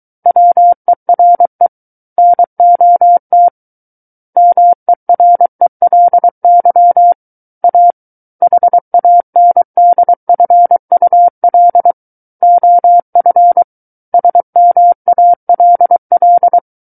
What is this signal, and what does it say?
Morse code, heard as a keyed tone, about 23 wpm, 700 Hz. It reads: WERE NOT MERELY A HANDFUL OF SMALL
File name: Book One / 14-Chapter 14 - 23 WPM